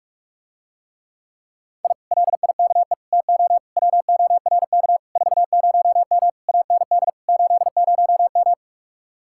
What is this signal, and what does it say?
Morse code, heard as a keyed tone, about 49 wpm, 700 Hz. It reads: I LIKE TO WORK 40M AND 80M